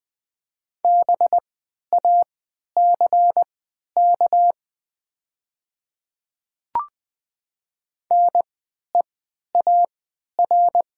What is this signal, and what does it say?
Morse code, heard as a keyed tone, about 20 wpm, spaced as for 11 wpm, 700 Hz. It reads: BACK NEAR
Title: 3x Character Spacing - Top 200 Words - 20wpm